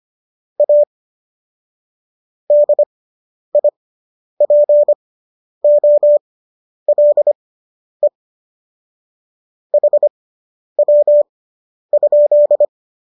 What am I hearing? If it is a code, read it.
Morse code, heard as a keyed tone, about 25 wpm, spaced as for 10 wpm, 590 Hz. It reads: A DIPOLE HW?